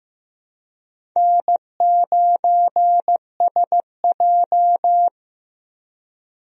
Morse code, heard as a keyed tone, about 15 wpm, 700 Hz. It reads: N9SJ
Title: USA Call Signs - 4 Characters - 15wpm